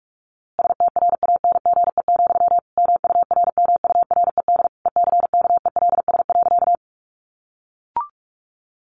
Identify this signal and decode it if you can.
Morse code, heard as a keyed tone, about 44 wpm, 700 Hz. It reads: STRANGE, MURMURED ECKELS.
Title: A Sound Of Thunder 45wpm